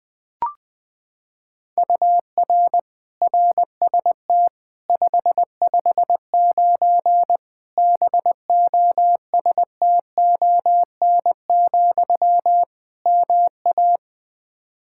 Morse code, heard as a keyed tone, about 20 wpm, 700 Hz. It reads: UR RST 559 BOSTON, MA
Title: K1USN SST QSO Practice 20wpm